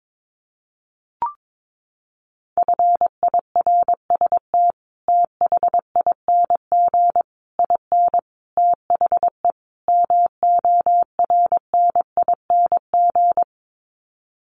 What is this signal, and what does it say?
Morse code, heard as a keyed tone, about 22 wpm, 700 Hz. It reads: FIRST THING IN THE MORNING
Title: Sets of 5 Words Encore2 22wpm